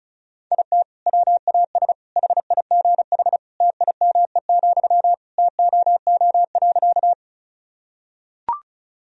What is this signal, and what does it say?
Morse code, heard as a keyed tone, about 34 wpm, 700 Hz. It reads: IT WAS HIGH TIME, TOO.
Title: Sentences from Top 1000 Words - Review Part 1 - 35wpm